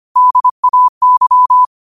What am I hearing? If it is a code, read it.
Morse code, heard as a keyed tone, about 25 wpm, 1000 Hz. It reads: DAY